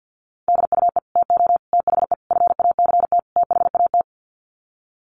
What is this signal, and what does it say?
Morse code, heard as a keyed tone, about 50 wpm, 700 Hz. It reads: DUE TO THE FACT THAT
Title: Sets of 5 Words Encore 50wpm